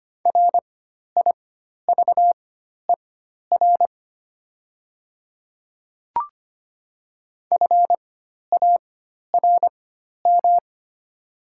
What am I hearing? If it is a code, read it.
Morse code, heard as a keyed tone, about 25 wpm, spaced as for 12 wpm, 700 Hz. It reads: RIVER FARM
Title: 4x Character Spacing - Top 500 Words - 25wpm